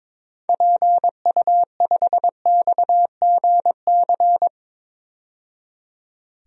Morse code, heard as a keyed tone, about 22 wpm, 700 Hz. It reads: PU5XGC
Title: DXCC-100 Call Signs - 6 Characters - 22wpm